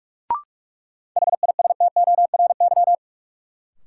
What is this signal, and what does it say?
Morse code, heard as a keyed tone, about 44 wpm, 700 Hz. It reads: HISTORY
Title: Sets of 3 Words - New Word Drill - 45wpm